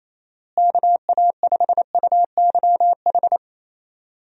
Morse code, heard as a keyed tone, about 27 wpm, 700 Hz. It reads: KA5UYH
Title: USA Call Signs - 6 Characters - Repeated 3x - 28wpm